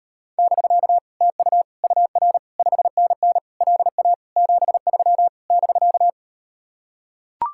Morse code, heard as a keyed tone, about 38 wpm, 700 Hz. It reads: <BK> TU UR 5NN LA 73 <BK>